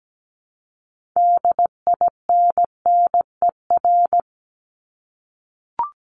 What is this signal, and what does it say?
Morse code, heard as a keyed tone, about 17 wpm, 700 Hz. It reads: DINNER
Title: Top 1751-2000 Words 17wpm